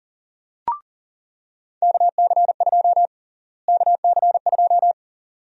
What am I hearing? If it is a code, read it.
Morse code, heard as a keyed tone, about 37 wpm, 700 Hz. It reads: KC2 KC2
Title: USA Call Sign Prefixes - Repeated 2x - 40wpm